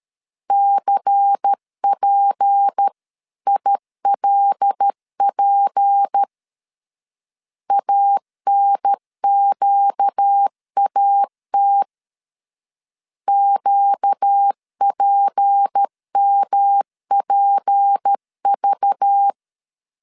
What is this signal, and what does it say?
Morse code, heard as a keyed tone, about 13 wpm, 800 Hz. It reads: CPILP ANQAT QPMPV
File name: LEZIONI / 6 PKX0 / 6 P 12wpm 2033